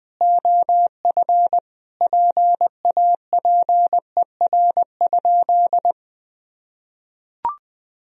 Morse code, heard as a keyed tone, about 20 wpm, 700 Hz. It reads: OF PAPER?
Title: Sentences from Top 1000 Words - Review Part 2 - 20wpm